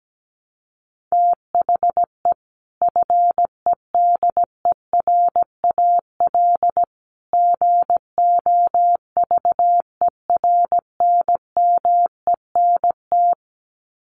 Morse code, heard as a keyed tone, about 17 wpm, 700 Hz. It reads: THE FEDERAL GOVERNMENT
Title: Sets of 3 Words 17wpm